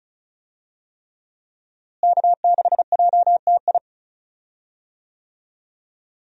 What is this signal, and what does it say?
Morse code, heard as a keyed tone, about 36 wpm, 700 Hz. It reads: K6JTI
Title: USA Call Signs - 5 Characters - Repeated 3x - 35wpm